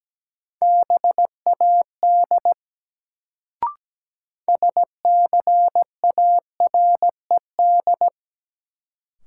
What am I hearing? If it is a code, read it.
Morse code, heard as a keyed tone, about 17 wpm, 700 Hz. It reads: BAD SCARED
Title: Sentences from Top 700 Words - New Word Drill - 17wpm